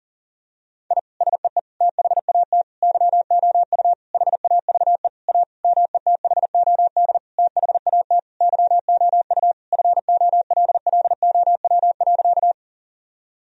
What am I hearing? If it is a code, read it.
Morse code, heard as a keyed tone, about 40 wpm, 700 Hz. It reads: I SEE THAT YOU HAVE A METHOD THAT YOU FOLLOW.
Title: Sentences from Top 1000 Words - Review Part 3 - 40wpm